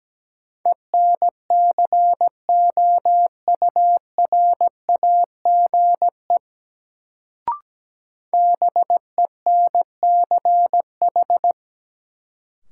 Morse code, heard as a keyed tone, about 17 wpm, 700 Hz. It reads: ENCOURAGE BENCH
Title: Sentences from Top 1400 Words - New Word Drill - 17wpm